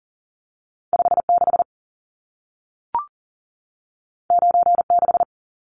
Morse code, heard as a keyed tone, about 40 wpm, 700 Hz. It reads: 56 96
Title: Numbers - 2-Digit 40wpm